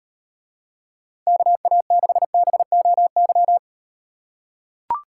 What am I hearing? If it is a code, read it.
Morse code, heard as a keyed tone, about 38 wpm, 700 Hz. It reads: KA6BOY